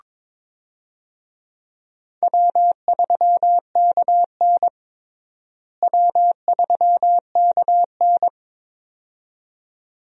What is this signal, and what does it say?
Morse code, heard as a keyed tone, about 21 wpm, 700 Hz. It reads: W3KN W3KN
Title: USA Call Signs - 4 Characters - Repeated 2x - 22wpm